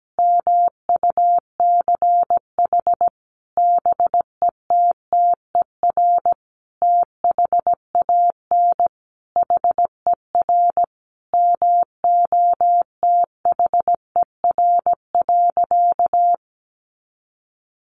Morse code, heard as a keyed tone, about 17 wpm, 700 Hz. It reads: MUCH BETTER THAN HER MOTHER.
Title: Sentences from Top 400 Words 17wpm